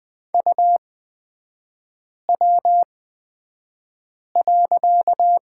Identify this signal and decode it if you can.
Morse code, heard as a keyed tone, about 20 wpm, spaced as for 5 wpm, 700 Hz. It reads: UW.
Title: 255 Lesson 41e - Period - Three Characters 20wpm